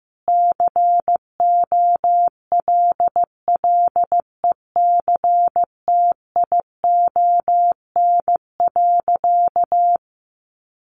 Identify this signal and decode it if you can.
Morse code, heard as a keyed tone, about 15 wpm, 700 Hz. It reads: COLLECTION.
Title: Sentences from Top 1200 Words 15wpm